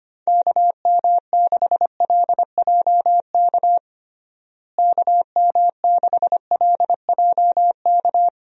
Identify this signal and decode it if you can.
Morse code, heard as a keyed tone, about 25 wpm, 700 Hz. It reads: KM6LJK KM6LJK
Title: USA Call Signs - 6 Characters - Repeated 2x - 25wpm